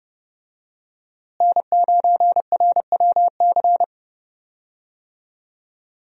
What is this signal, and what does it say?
Morse code, heard as a keyed tone, about 32 wpm, 700 Hz. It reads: N9RWC
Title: USA Call Signs - 5 Characters - Repeated 3x - 30wpm